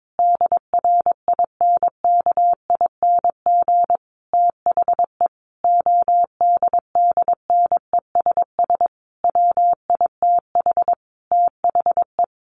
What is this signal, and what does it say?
Morse code, heard as a keyed tone, about 22 wpm, 700 Hz. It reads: DRINKING THE ODDNESS WITH THE
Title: A Sound Of Thunder 22wpm